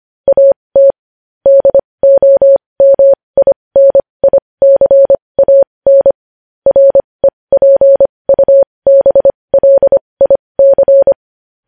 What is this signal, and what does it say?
Morse code, heard as a keyed tone, about 25 wpm, spaced as for 20 wpm, 550 Hz. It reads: AT DOMINICAN REPUBLIC
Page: news headlines in morse code at 20 wpm